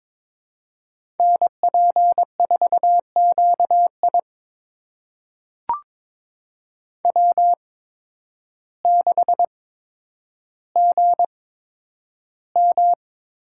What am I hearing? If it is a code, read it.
Morse code, heard as a keyed tone, about 23 wpm, spaced as for 10 wpm, 700 Hz. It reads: NP4QI W6GM
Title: USA Call Signs - 5 Characters - 8x Character Spacing - 22wpm